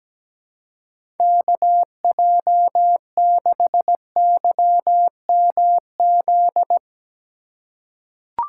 Morse code, heard as a keyed tone, about 17 wpm, 700 Hz. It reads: KJ6YMZ
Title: USA Call Signs - 3x Character Spacing - 17wpm